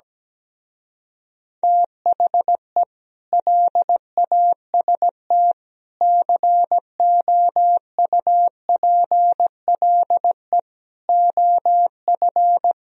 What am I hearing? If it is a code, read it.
Morse code, heard as a keyed tone, about 17 wpm, 700 Hz. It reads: THE LAST COUPLE OF
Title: Speed-Racing - Sets of 5 Words 17wpm